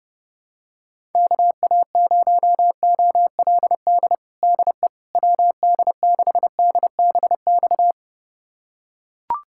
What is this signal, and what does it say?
Morse code, heard as a keyed tone, about 30 wpm, 700 Hz. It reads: KA0OLD DE WD6DBX
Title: USA Callsign DE Callsign - 6 Characters - 30wpm